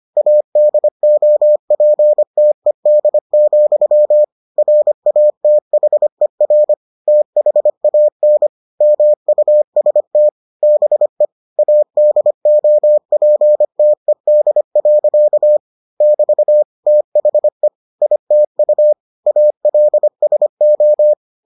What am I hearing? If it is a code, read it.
Morse code, heard as a keyed tone, about 25 wpm, 600 Hz. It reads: ADOPTED, RATHER THAN MUST BE ADOPTED. <BT> THE ITU ALSO